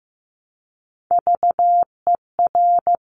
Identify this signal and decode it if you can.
Morse code, heard as a keyed tone, about 15 wpm, 700 Hz. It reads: VER
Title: Top Three Letter Combinations in Words 15wpm